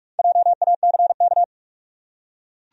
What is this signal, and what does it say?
Morse code, heard as a keyed tone, about 46 wpm, 700 Hz. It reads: JACK
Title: USA Common Names - Max 4 Letters - 45wpm